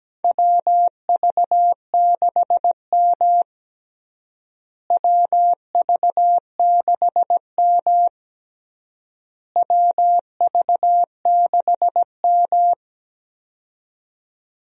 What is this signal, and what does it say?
Morse code, heard as a keyed tone, about 17 wpm, 700 Hz. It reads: WV6M WV6M WV6M
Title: USA Call Signs - 4 Characters - Repeated 3x - 17wpm